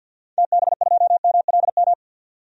Morse code, heard as a keyed tone, about 50 wpm, 700 Hz. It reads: TB2MLM
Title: DXCC-100 Call Signs - 6 Characters - 50wpm